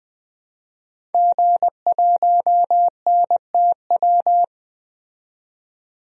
Morse code, heard as a keyed tone, about 20 wpm, 700 Hz. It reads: G1NTW
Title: DXCC-100 Call Signs - 5 Characters - 20wpm